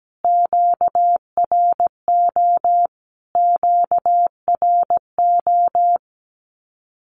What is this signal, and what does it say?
Morse code, heard as a keyed tone, about 17 wpm, 700 Hz. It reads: QRO QRO
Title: Q-Codes Repeated 3x 17wpm